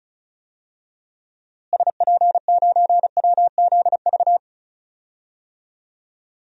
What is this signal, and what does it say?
Morse code, heard as a keyed tone, about 36 wpm, 700 Hz. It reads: SP9WZV
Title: DXCC-100 Call Signs - 6 Characters - Repeated 2x - 35wpm